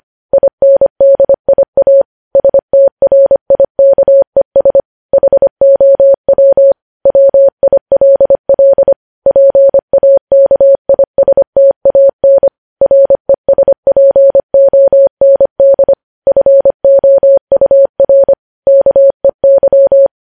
Morse code, heard as a keyed tone, about 25 wpm, 550 Hz. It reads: INDIA STRIKES HOW WILL PAKISTAN RESPOND FOUR KEY
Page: news headlines in morse code at 25 wpm